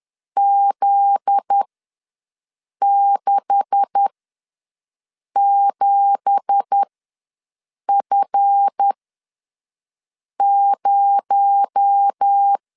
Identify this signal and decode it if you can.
Morse code, heard as a keyed tone, about 10 wpm, spaced as for 5 wpm, 800 Hz. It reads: Z67F0
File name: LEZIONI / 9 2867 ws 10wpm 4215